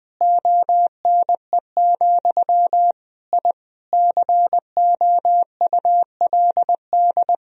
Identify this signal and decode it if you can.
Morse code, heard as a keyed tone, about 20 wpm, 700 Hz. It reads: ONE, I COULD
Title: Sentences from Top 1000 Words - Review Part 3 - 20wpm